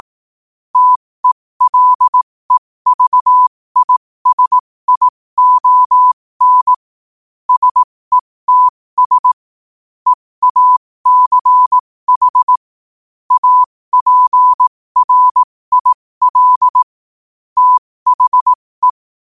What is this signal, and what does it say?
Morse code, heard as a keyed tone, about 18 wpm, spaced as for 15 wpm, 1000 Hz. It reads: TELEVISION SETS EACH APRIL THE
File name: qotdfiles / 15wpm